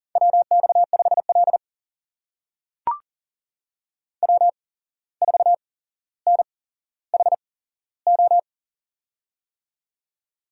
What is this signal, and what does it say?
Morse code, heard as a keyed tone, about 40 wpm, spaced as for 16 wpm, 700 Hz. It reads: WX5L W4NHO
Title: USA Call Signs - 8x Character Spacing - 40wpm